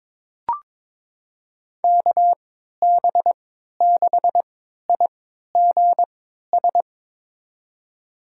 Morse code, heard as a keyed tone, about 22 wpm, spaced as for 12 wpm, 700 Hz. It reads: KB6IGS
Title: USA Call Signs - 6 Characters - 3x Character Spacing - 22wpm